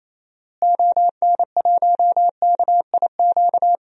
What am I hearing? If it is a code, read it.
Morse code, heard as a keyed tone, about 28 wpm, 700 Hz. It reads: ON1KIQ
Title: DXCC-100 Call Signs - 6 Characters - Repeated 2x - 28wpm